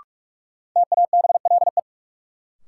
Morse code, heard as a keyed tone, about 45 wpm, 700 Hz. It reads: TABLE
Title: Sentences from Top 400 Words - New Word Drill - 45wpm